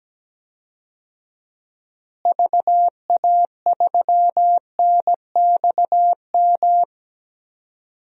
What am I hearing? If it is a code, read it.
Morse code, heard as a keyed tone, about 17 wpm, 700 Hz. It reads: VA3NXM
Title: DXCC-100 Call Signs - 6 Characters - Repeated 2x - 17wpm